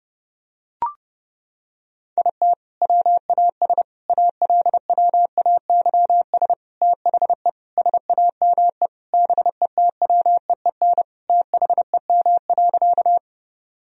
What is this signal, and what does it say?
Morse code, heard as a keyed tone, about 29 wpm, 700 Hz. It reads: IT WAS ALWAYS THE SAME BETWEEN THEM.
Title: Sentences from Top 1000 Words - Review Part 1 - 30wpm